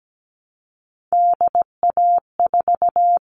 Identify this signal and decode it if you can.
Morse code, heard as a keyed tone, about 17 wpm, 700 Hz. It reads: DA4
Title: All Cut Numbers - 3-Digit 17wpm